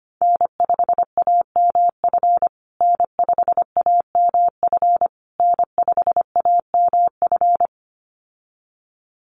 Morse code, heard as a keyed tone, about 25 wpm, 700 Hz. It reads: N5AMF N5AMF N5AMF
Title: USA Call Signs Repeated 3x 25wpm